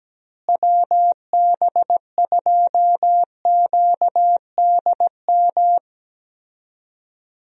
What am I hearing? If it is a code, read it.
Morse code, heard as a keyed tone, about 17 wpm, 700 Hz. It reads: WB2QDM
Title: USA Call Signs - 6 Characters - Repeated 3x - 17wpm